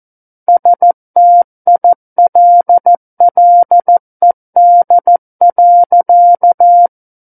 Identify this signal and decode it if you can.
Morse code, heard as a keyed tone, about 14 wpm, 700 Hz. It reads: STILLED.
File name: Book One / 05-Chapter 5 - 14 WPM